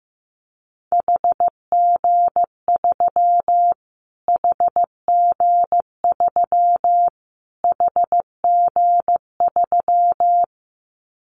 Morse code, heard as a keyed tone, about 15 wpm, 700 Hz. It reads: HG3 HG3 HG3
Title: Callsign Prefixes for Top 25 DXCC - Repeated 3x - 15wpm